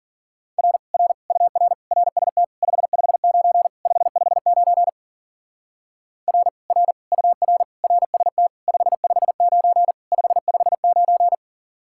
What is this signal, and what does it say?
Morse code, heard as a keyed tone, about 48 wpm, 700 Hz. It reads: R R UR RST 559 559 R R UR RST 559 559
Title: Speed-Racing Common QSO Phrases Encore - 40wpm